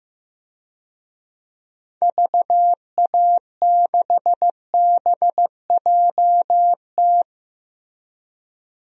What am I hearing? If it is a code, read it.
Morse code, heard as a keyed tone, about 15 wpm, 700 Hz. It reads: VA6BJT
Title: DXCC-100 Call Signs - 6 Characters - Repeated 2x - 15wpm